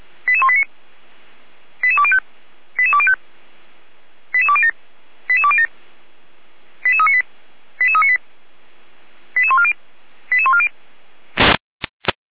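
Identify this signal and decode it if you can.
Five-tone selective call (ZVEI), heard as quick runs of five tones, tones of about 70 ms. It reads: 88189 88205 88205 88207 88207 88398 88398 88140 88140
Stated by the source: ZVEI five-tone call received by SDR, code 88179.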